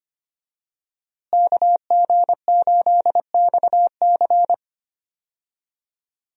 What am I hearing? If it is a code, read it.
Morse code, heard as a keyed tone, about 25 wpm, 700 Hz. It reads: KG8XC
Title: USA Call Signs - 5 Characters - Repeated 3x - 25wpm